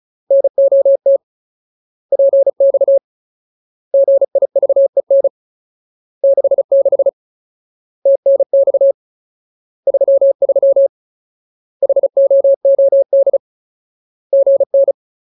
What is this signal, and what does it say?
Morse code, heard as a keyed tone, about 34 wpm, 550 Hz. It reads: NOT PX GIVEN 66 TNX 33 HOOD GN
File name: Ultimate Word List / 35WPM Ultimate